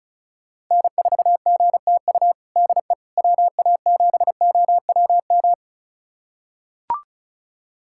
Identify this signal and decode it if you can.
Morse code, heard as a keyed tone, about 36 wpm, 700 Hz. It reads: N4GTU DE WA7OWM
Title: USA Callsign DE Callsign 35wpm